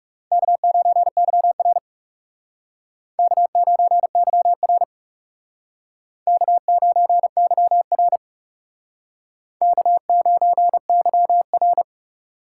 Morse code, heard as a keyed tone, about 39 wpm, 700 Hz. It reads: K9YR K9YR K9YR K9YR